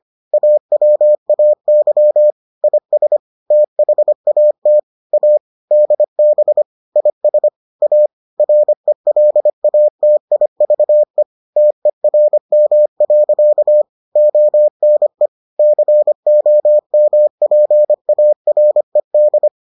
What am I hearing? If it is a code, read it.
Morse code, heard as a keyed tone, about 25 wpm, 600 Hz. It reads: AWAY IS THAT A DB IS A RELATIVE TERM. ONE COMPARED